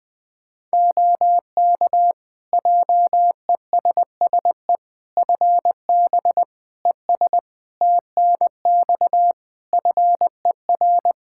Morse code, heard as a keyed tone, about 20 wpm, 700 Hz. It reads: OK JESSE FB ES TNX FER
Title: Practice QSO - Exchange 2 - Info - 20wpm